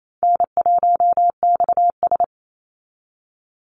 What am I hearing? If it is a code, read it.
Morse code, heard as a keyed tone, about 28 wpm, 700 Hz. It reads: N1XS
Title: Easy Call Signs Repeated 2x - 28wpm